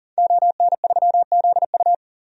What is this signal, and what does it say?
Morse code, heard as a keyed tone, about 40 wpm, 700 Hz. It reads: ON3ZU
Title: DXCC-100 Call Signs - 5 Characters - 40wpm